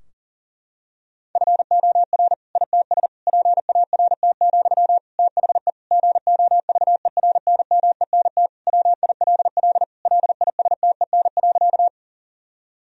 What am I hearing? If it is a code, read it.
Morse code, heard as a keyed tone, about 40 wpm, 700 Hz. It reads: FOR ITS PART, THE GOVERNMENT WILL LISTEN.
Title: Sentences from Top 1000 Words - Review Part 3 - 40wpm